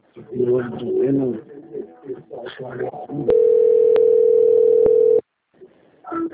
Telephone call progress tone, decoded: ringback tone

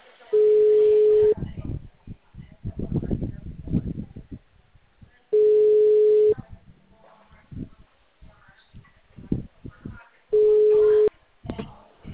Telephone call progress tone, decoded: ringback tone

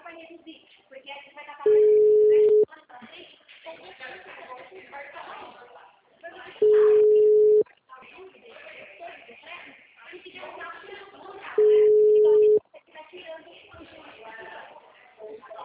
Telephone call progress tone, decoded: ringback tone

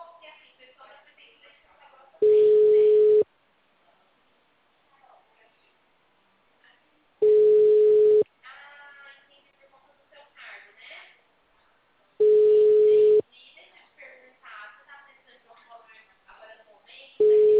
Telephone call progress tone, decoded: ringback tone

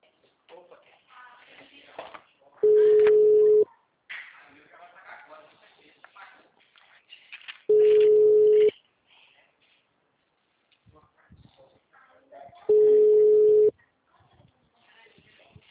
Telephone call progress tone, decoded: ringback tone

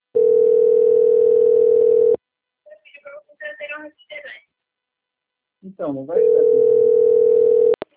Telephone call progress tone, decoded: ringback tone

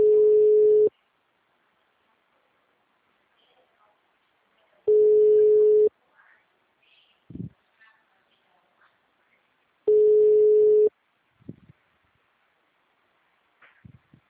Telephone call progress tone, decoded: ringback tone

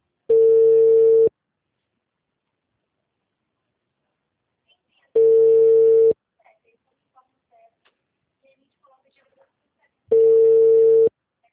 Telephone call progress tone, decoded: ringback tone

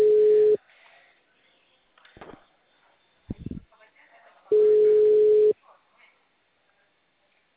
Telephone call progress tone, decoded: ringback tone

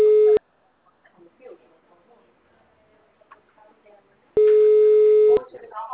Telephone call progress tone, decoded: ringback tone

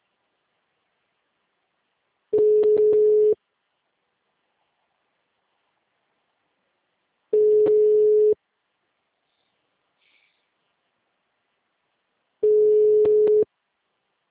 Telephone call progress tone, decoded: ringback tone

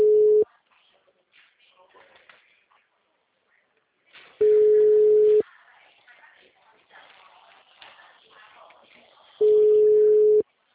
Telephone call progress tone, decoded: ringback tone

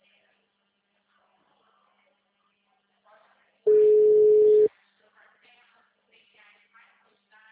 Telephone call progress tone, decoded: ringback tone

